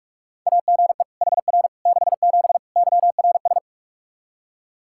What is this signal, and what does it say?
Morse code, heard as a keyed tone, about 46 wpm, 700 Hz. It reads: AGE HR 67 YRS
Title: Practice QSO - Exchange 3 - More Info - 45wpm